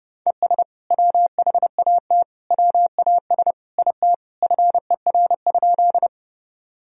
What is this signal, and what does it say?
Morse code, heard as a keyed tone, about 30 wpm, 700 Hz. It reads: ES WHAT WAS IT FER?
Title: Sentences from Top 100 Words and Abbreviations 30wpm